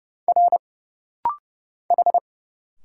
Morse code, heard as a keyed tone, about 29 wpm, 700 Hz. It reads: R H
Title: Single Letters - Rapid-Fire 30wpm